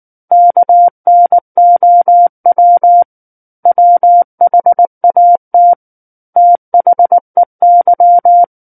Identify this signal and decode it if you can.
Morse code, heard as a keyed tone, about 19 wpm, 700 Hz. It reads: KNOW WHAT THEY
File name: Book One / 10-Chapter 10 - 19 WPM